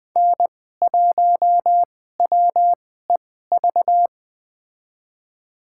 Morse code, heard as a keyed tone, about 20 wpm, spaced as for 14 wpm, 700 Hz. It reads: N1WEV
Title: USA Call Signs - 5 Characters - 2x Character Spacing - 20wpm